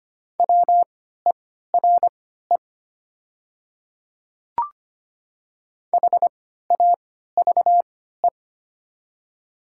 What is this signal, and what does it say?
Morse code, heard as a keyed tone, about 25 wpm, spaced as for 14 wpm, 700 Hz. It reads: WERE HAVE
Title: 3x Character Spacing - Top 100 Words - 25wpm